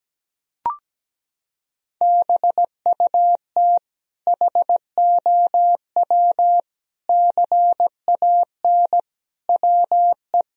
Morse code, heard as a keyed tone, about 17 wpm, 700 Hz. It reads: BUT HOW CAN WE
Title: Sentences from Top 1000 Words - Review Part 4 - 17wpm